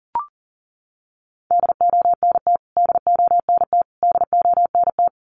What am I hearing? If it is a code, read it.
Morse code, heard as a keyed tone, about 40 wpm, 700 Hz. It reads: DONT DONT DONT
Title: Top 300 Words Repeated 40wpm